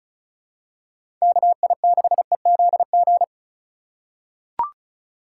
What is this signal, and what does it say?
Morse code, heard as a keyed tone, about 36 wpm, 700 Hz. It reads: KI6EZG
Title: USA Call Signs - 6 Characters - 2x Character Spacing - 35wpm